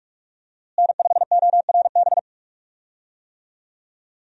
Morse code, heard as a keyed tone, about 42 wpm, 700 Hz. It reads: N5ORB
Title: USA Call Signs - 5 Characters - Repeated 2x - 45wpm